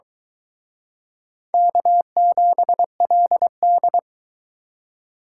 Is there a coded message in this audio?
Yes: Morse code from 1.5 to 4.0 s, heard as a keyed tone, about 23 wpm, 700 Hz. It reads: K7LD